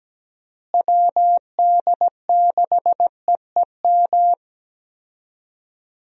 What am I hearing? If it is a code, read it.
Morse code, heard as a keyed tone, about 17 wpm, 700 Hz. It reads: WD6EEM
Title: USA Call Signs - 6 Characters - Repeated 2x - 17wpm